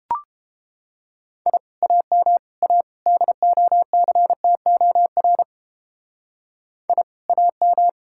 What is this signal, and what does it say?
Morse code, heard as a keyed tone, about 29 wpm, 700 Hz. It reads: I AM A DOCTOR I AM